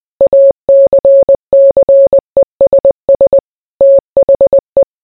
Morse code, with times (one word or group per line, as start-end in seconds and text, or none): 0.2-3.4 ACCESS
3.8-4.8 THE